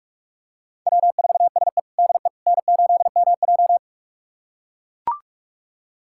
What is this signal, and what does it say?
Morse code, heard as a keyed tone, about 43 wpm, 700 Hz. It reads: W4SE DE N8MJ
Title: USA Callsign DE Callsign - 4 Characters - 45wpm